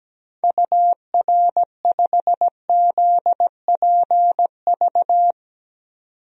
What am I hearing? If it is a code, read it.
Morse code, heard as a keyed tone, about 17 wpm, 700 Hz. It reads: UR5ZPV
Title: DXCC-100 Call Signs - 6 Characters - 17wpm